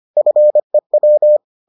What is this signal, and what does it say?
Morse code, heard as a keyed tone, about 25 wpm, 600 Hz. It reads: FEW